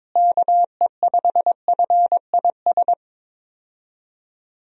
Morse code, heard as a keyed tone, about 22 wpm, 700 Hz. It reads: KE5FIS
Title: USA Call Signs - 6 Characters - Repeated 2x - 22wpm